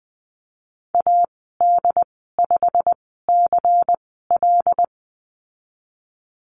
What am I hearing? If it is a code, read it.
Morse code, heard as a keyed tone, about 20 wpm, spaced as for 14 wpm, 700 Hz. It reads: AD5CL
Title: 128 Lesson 21g - Number 6 - Call Signs 20wpm